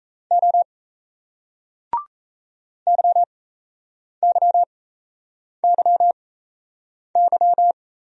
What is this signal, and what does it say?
Morse code, heard as a keyed tone, about 38 wpm, spaced as for 8 wpm, 700 Hz. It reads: O YYYY